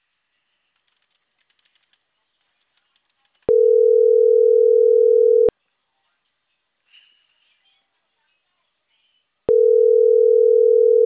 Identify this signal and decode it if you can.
Ringback tone (North America), 440 + 480 Hz.